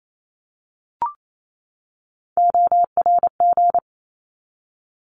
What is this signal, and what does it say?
Morse code, heard as a keyed tone, about 28 wpm, 700 Hz. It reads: ORG
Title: ARRL Sections 28wpm